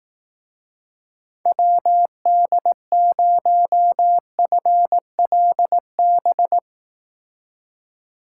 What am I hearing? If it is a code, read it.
Morse code, heard as a keyed tone, about 18 wpm, 700 Hz. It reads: WD0FLB
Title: Speed Racing - USA 2x3 Call Signs 15wpm